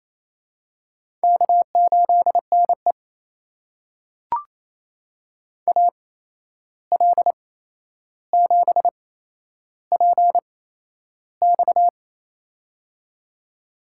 Morse code, heard as a keyed tone, about 28 wpm, spaced as for 11 wpm, 700 Hz. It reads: K8NE AL7PX
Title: USA Call Signs - 8x Character Spacing - 28wpm